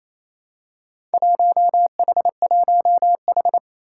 Morse code, heard as a keyed tone, about 28 wpm, 700 Hz. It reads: 1H1H
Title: Four Challenging Letter-Number Combinations 28wpm